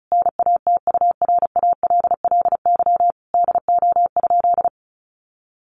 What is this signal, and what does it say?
Morse code, heard as a keyed tone, about 34 wpm, 700 Hz. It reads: NATURALLY DO?
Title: Sentences from Top 1300 Words 35wpm